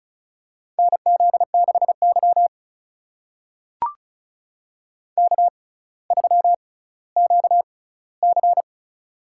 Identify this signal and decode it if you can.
Morse code, heard as a keyed tone, about 36 wpm, spaced as for 18 wpm, 700 Hz. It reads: NZ6Y K3QC
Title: USA Call Signs - 4 Characters - 6x Character Spacing - 35wpm